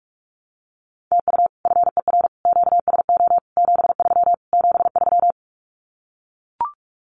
Morse code, heard as a keyed tone, about 44 wpm, 700 Hz. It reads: TU FER QSO 73 73
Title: Common QSO Phrases Encore 45wpm